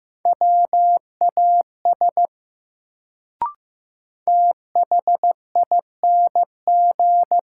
Morse code, heard as a keyed tone, about 15 wpm, 700 Hz. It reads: WAS THING
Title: Sentences from Top 100 Words - New Word Drill - 15wpm